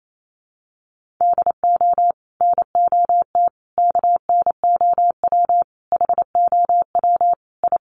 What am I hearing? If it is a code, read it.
Morse code, heard as a keyed tone, about 28 wpm, 700 Hz. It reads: DO NOT KNOW HOW I
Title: Sets of 5 Words Encore2 28wpm